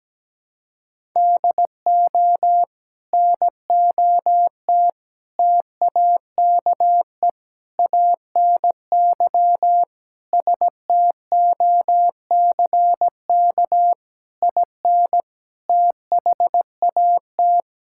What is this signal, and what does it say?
Morse code, heard as a keyed tone, about 17 wpm, 700 Hz. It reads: DO NOT TAKE ANY STOCK IN THAT
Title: Sentences from Top 1000 Words - Review Part 5 - 17wpm